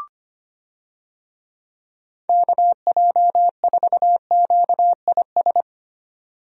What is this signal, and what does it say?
Morse code, heard as a keyed tone, about 25 wpm, 700 Hz. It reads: KJ4QIS
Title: USA Call Signs - 6 Characters - 25wpm